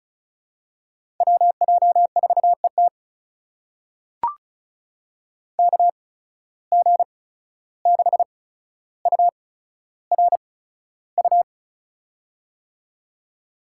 Morse code, heard as a keyed tone, about 36 wpm, spaced as for 14 wpm, 700 Hz. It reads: WJ4ET KG6URU
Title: USA Call Signs - 8x Character Spacing - 35wpm